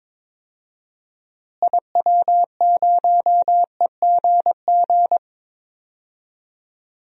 Morse code, heard as a keyed tone, about 22 wpm, 700 Hz. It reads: IW0EGG
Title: DXCC-100 Call Signs - 6 Characters - Repeated 2x - 22wpm